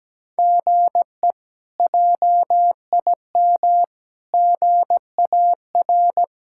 Morse code, heard as a keyed tone, about 17 wpm, 700 Hz. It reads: GE JIM GAR
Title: K1USN SST QSO Practice - 17wpm